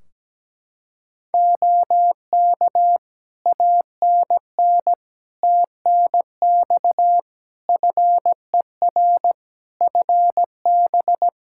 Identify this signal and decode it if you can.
Morse code, heard as a keyed tone, about 17 wpm, 700 Hz. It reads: OK ANN TNX FER FB
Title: K1USN SST QSO Practice 17wpm